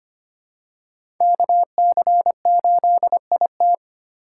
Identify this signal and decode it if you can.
Morse code, heard as a keyed tone, about 25 wpm, 700 Hz. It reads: KC8IT
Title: USA Call Signs - 5 Characters - 6x Character Spacing - 25wpm